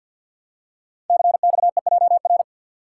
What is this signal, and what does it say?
Morse code, heard as a keyed tone, about 50 wpm, 700 Hz. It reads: CXEJR
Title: Code Groups - 5 Letter:Numbers - 50wpm